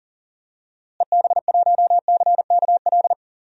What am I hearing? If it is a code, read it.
Morse code, heard as a keyed tone, about 40 wpm, 700 Hz. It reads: EB1CKL
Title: DXCC-100 Call Signs - 6 Characters - Repeated 2x - 40wpm